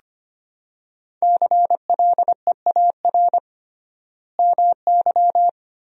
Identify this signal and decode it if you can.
Morse code, heard as a keyed tone, about 25 wpm, 700 Hz. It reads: CLEAR MY
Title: 3x Word-Spacing - Sentences from Top 500 Words - 25wpm